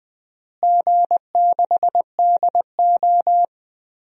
Morse code, heard as a keyed tone, about 20 wpm, 700 Hz. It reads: G6DO
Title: DXCC-100 Call Signs - 4 Characters - Repeated 3x - 20wpm